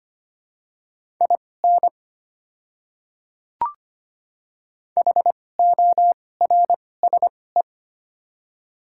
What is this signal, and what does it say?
Morse code, heard as a keyed tone, about 25 wpm, spaced as for 18 wpm, 700 Hz. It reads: IN HORSE
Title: 2x Character Spacing - Top 300 Words - 25wpm